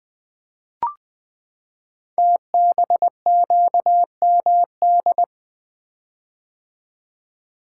Morse code, heard as a keyed tone, about 20 wpm, 700 Hz. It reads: TBQMD
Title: Code Groups - 5 Letters - 20wpm